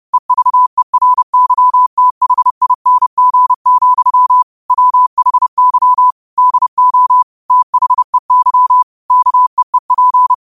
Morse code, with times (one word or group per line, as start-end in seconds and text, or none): 0.1-4.4 EVERYTHING,
4.7-6.1 WHY
6.4-7.2 DO
7.5-8.8 THEY
9.1-10.3 KEEP